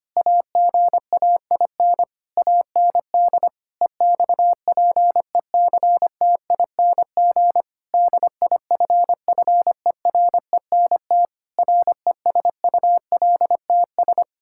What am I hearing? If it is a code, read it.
Morse code, heard as a keyed tone, about 25 wpm, 700 Hz. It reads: AGAIN AND EXPECTING DIFFERENT RESULTS
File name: practice / 25wpm / Sentences with 9-Word Phrases - 25wpm